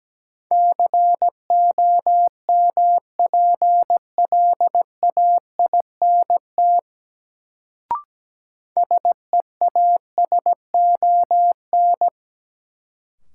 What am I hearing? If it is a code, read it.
Morse code, heard as a keyed tone, about 17 wpm, 700 Hz. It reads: COMPLAINT SEASON
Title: Sentences from Top 1400 Words - New Word Drill Alt - 17wpm